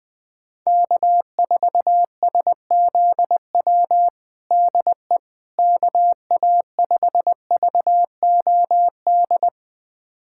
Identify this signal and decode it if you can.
Morse code, heard as a keyed tone, about 20 wpm, 700 Hz. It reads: K4SZW DE KA5VOD
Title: USA Callsign DE Callsign 20wpm